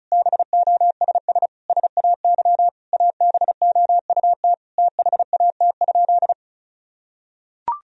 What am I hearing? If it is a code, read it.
Morse code, heard as a keyed tone, about 36 wpm, 700 Hz. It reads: BOSS SAY ABOUT THAT?
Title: Sentences from Top 1000 Words - Review Part 4 - 35wpm